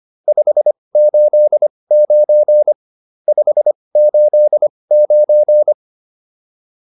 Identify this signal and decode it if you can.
Morse code, heard as a keyed tone, about 25 wpm, spaced as for 20 wpm, 600 Hz. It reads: 589 589